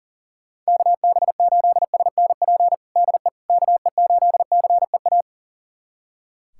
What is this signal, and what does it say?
Morse code, heard as a keyed tone, about 40 wpm, 700 Hz. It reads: KB8SNP DE KE8CEA
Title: USA Callsign DE Callsign - 6 Characters - 40wpm